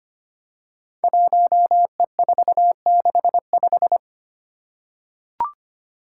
Morse code, heard as a keyed tone, about 25 wpm, 700 Hz. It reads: 1E465
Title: Code Groups - 5 Letter:Numbers - 25wpm